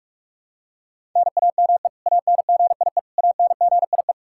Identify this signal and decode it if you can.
Morse code, heard as a keyed tone, about 46 wpm, 700 Hz. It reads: NAME ANGIE ANGIE
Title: Practice QSO - Exchange 1 - Beginning - 45wpm